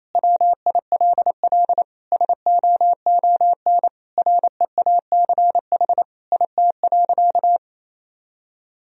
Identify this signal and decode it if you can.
Morse code, heard as a keyed tone, about 27 wpm, 700 Hz. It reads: WILL SOON REACH IT.
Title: Sentences from Top 400 Words and Abbreviations 28wpm